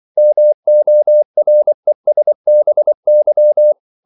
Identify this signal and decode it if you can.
Morse code, heard as a keyed tone, about 24 wpm, 600 Hz. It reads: MORESBY
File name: CW / Practice / NationalCapitals24